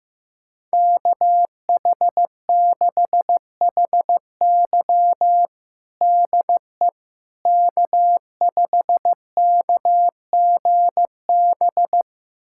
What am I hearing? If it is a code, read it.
Morse code, heard as a keyed tone, about 15 wpm, 700 Hz. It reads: KH6HY DE K5KGB
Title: USA Callsign DE Callsign - 5 Characters - 15wpm